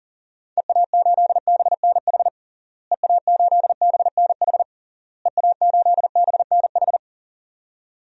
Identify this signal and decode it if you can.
Morse code, heard as a keyed tone, about 37 wpm, 700 Hz. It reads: EA8BNH EA8BNH EA8BNH
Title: DXCC-100 Call Signs - 6 Characters - Repeated 3x - 40wpm